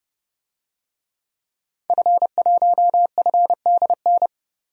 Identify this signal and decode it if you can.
Morse code, heard as a keyed tone, about 30 wpm, 700 Hz. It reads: F1FDN
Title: DXCC-100 Call Signs - 5 Characters - 30wpm